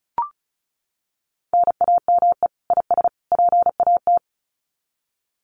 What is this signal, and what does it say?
Morse code, heard as a keyed tone, about 34 wpm, 700 Hz. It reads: NAME IS PAT
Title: Common QSO Phrases 35wpm